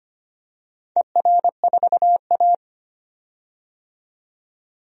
Morse code, heard as a keyed tone, about 25 wpm, 700 Hz. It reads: ER4A
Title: DXCC-100 Call Signs - 4 Characters - 25wpm